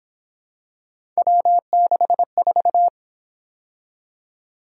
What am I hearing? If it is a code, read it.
Morse code, heard as a keyed tone, about 28 wpm, 700 Hz. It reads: W64